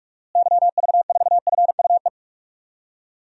Morse code, heard as a keyed tone, about 43 wpm, 700 Hz. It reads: YV4FUE
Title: DXCC-100 Call Signs - 6 Characters - Repeated 2x - 45wpm